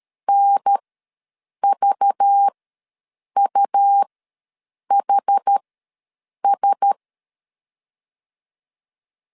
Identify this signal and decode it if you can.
Morse code, heard as a keyed tone, about 13 wpm, spaced as for 7 wpm, 800 Hz. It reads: NVUHS
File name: LEZIONI / 5 LBVC / 5 LBV ws 12wpm 3015